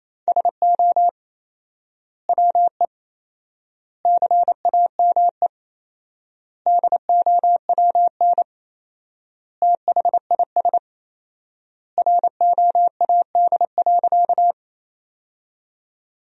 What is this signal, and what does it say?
Morse code, heard as a keyed tone, about 28 wpm, 700 Hz. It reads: SO WE CAME DOWN THIS ROAD.
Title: 4x Word-Spacing - Sentences from Top 400 Words - 28wpm